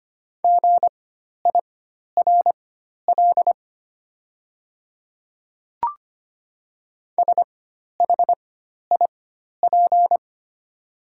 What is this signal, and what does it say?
Morse code, heard as a keyed tone, about 25 wpm, spaced as for 12 wpm, 700 Hz. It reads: GIRL SHIP
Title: 4x Character Spacing - Top 500 Words - 25wpm